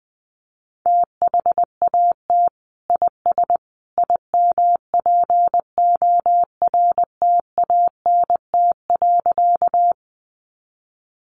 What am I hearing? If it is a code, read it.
Morse code, heard as a keyed tone, about 20 wpm, 700 Hz. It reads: THAT IS IMPORTANT.
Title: Sentences from Top 600 Words 20wpm